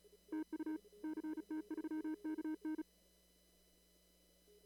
Morse code, heard as a keyed tone, about 35 wpm, 310 Hz. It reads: TU CT3KN
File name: CW / kt5j wpx cw-op-k5pi